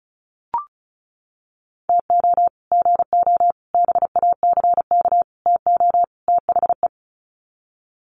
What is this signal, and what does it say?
Morse code, heard as a keyed tone, about 34 wpm, 700 Hz. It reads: TO GO BACK TO THE
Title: Sets of 5 Words 35wpm